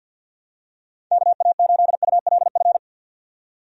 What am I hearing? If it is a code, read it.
Morse code, heard as a keyed tone, about 49 wpm, 700 Hz. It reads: KA8ULF
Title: USA Call Signs - 6 Characters - 50wpm